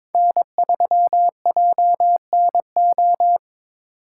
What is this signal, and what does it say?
Morse code, heard as a keyed tone, about 27 wpm, 700 Hz. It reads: N3JNO
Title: Speed Racing - USA Call Signs 22wpm